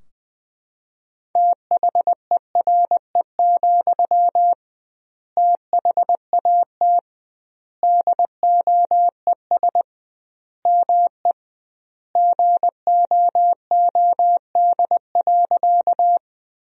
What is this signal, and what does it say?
Morse code, heard as a keyed tone, about 20 wpm, 700 Hz. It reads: THERE, THAT DOES ME GOOD.
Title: 2x Word-Spacing - Sentences from Top 200 Words - 20wpm